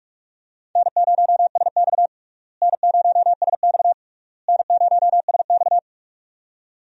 Morse code, heard as a keyed tone, about 41 wpm, 700 Hz. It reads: N0SX N0SX N0SX
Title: USA Call Signs - 4 Characters - Repeated 3x - 45wpm